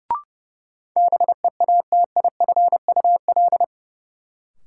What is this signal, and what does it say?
Morse code, heard as a keyed tone, about 30 wpm, 700 Hz. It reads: BEATIFUL
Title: Sentences from Top 1300 Words - New Word Drill - 30wpm